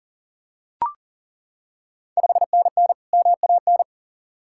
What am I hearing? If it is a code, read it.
Morse code, heard as a keyed tone, about 40 wpm, 690 Hz. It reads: 5NN MAN
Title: Ontario QSO Party Exchange 40wpm